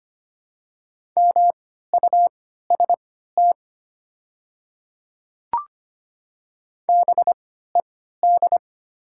Morse code, heard as a keyed tone, about 25 wpm, spaced as for 14 wpm, 690 Hz. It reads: MUST BED
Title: 3x Character Spacing - Top 500 Words - 25wpm